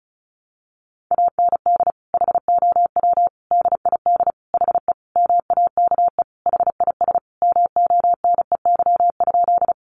Morse code, heard as a keyed tone, about 34 wpm, 700 Hz. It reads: AND HOW DID HE MAKE HIS MONEY?
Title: Sentences from Top 400 Words 35wpm